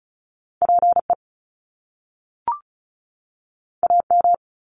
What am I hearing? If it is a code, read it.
Morse code, heard as a keyed tone, about 35 wpm, 700 Hz. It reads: PE AM
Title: Top Two Letter Combinations in Words 35wpm